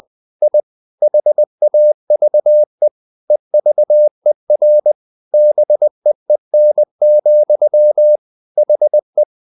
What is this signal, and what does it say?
Morse code, heard as a keyed tone, about 20 wpm, 600 Hz. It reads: I HAVE EVER BEEN, HE
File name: the old man and the sea part 2 20wpm